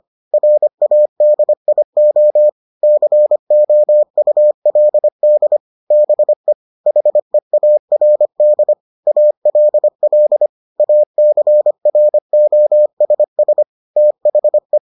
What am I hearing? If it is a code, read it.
Morse code, heard as a keyed tone, about 25 wpm, 600 Hz. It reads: RADIO COULD BE HEARD ALL ACROSS THE